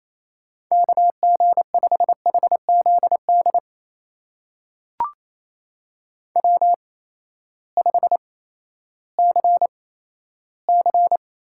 Morse code, heard as a keyed tone, about 28 wpm, spaced as for 14 wpm, 700 Hz. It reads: KG5HZD W5CC
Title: USA Call Signs - 8x Character Spacing - 28wpm